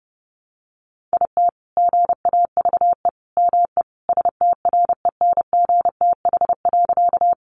Morse code, heard as a keyed tone, about 30 wpm, 700 Hz. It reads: IT GAVE ME STRENGTH.
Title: Sentences from Top 1300 Words 30wpm